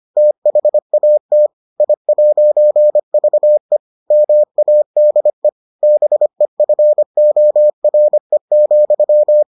Morse code, heard as a keyed tone, about 25 wpm, 600 Hz. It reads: THAT I'VE MADE BEFORE,